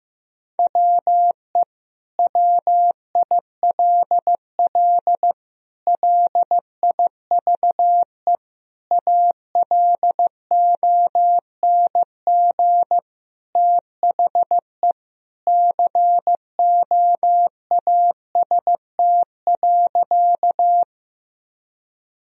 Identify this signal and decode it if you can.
Morse code, heard as a keyed tone, about 15 wpm, 700 Hz. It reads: WE WILL LIVE ALONG THE COAST.
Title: Sentences from Top 1000 Words - Review Part 4 - 15wpm